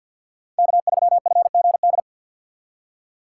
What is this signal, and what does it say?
Morse code, heard as a keyed tone, about 51 wpm, 700 Hz. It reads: K3FGD
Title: USA Call Signs - 5 Characters - 3x Character Spacing - 50wpm